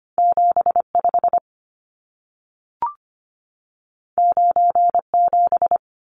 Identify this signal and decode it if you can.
Morse code, heard as a keyed tone, about 25 wpm, 700 Hz. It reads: 75 97
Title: Numbers - 2-Digit 25wpm